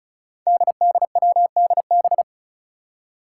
Morse code, heard as a keyed tone, about 36 wpm, 700 Hz. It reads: DDWDB
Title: Five Challenging Letter-Number Combinations 35wpm